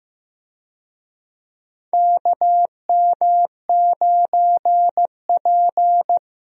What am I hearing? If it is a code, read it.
Morse code, heard as a keyed tone, about 15 wpm, 700 Hz. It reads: KM9P
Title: USA Call Signs - 4 Characters - Repeated 2x - 15wpm